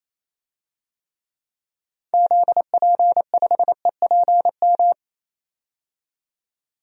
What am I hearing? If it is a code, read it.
Morse code, heard as a keyed tone, about 28 wpm, 700 Hz. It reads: ZP5EPM
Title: DXCC-100 Call Signs - 6 Characters - 28wpm